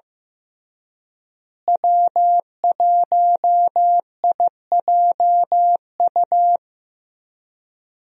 Morse code, heard as a keyed tone, about 15 wpm, 700 Hz. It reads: W1IJU
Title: USA Call Signs - 5 Characters - Repeated 3x - 15wpm